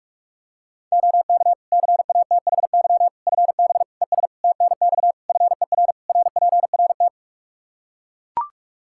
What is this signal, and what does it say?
Morse code, heard as a keyed tone, about 41 wpm, 700 Hz. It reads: OK CATHY FB ES TNX FER RPRT
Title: Practice QSO - Exchange 2 - Info - 45wpm